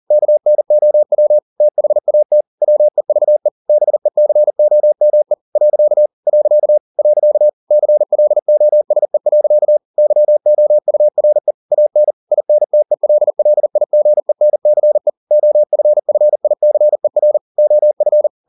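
Morse code, heard as a keyed tone, about 40 wpm, 600 Hz. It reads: KNOW THAT WEVE BECOME . . CLOSE. YOURE AN INTELLIGENCE OFFICER OF
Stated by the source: Morse code at 40 wpm, practice text, CW med Gnister.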